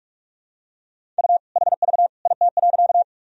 Morse code, heard as a keyed tone, about 42 wpm, 700 Hz. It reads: U HV IT.
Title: Sentences from Top 100 Words and Abbreviations 45wpm